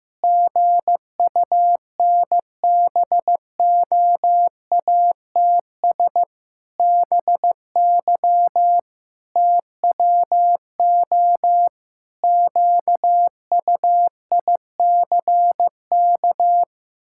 Morse code, heard as a keyed tone, about 15 wpm, 700 Hz. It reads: GUNBOATS BY TWO QUICK
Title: Pangram Challenge v2 - 15wpm